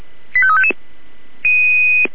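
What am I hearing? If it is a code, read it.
Five-tone selective call (ZVEI), heard as quick runs of five tones, tones of about 70 ms. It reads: 85355
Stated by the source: ZVEI five-tone selective call received by SDR.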